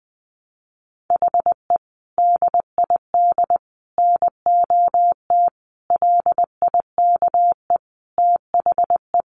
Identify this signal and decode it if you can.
Morse code, heard as a keyed tone, about 20 wpm, 700 Hz. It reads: HE DID NOT LIKE THE
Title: Sentences from Top 1200 Words 20wpm